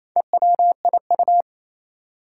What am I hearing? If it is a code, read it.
Morse code, heard as a keyed tone, about 28 wpm, 700 Hz. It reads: EWIU
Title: Four Challenging Letter-Number Combinations 28wpm